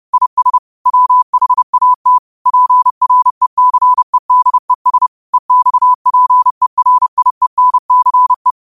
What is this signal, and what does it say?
Morse code, heard as a keyed tone, about 30 wpm, 1000 Hz. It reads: IS WHAT PRECEDES EXPERIENCE
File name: qotdfiles / 30wpm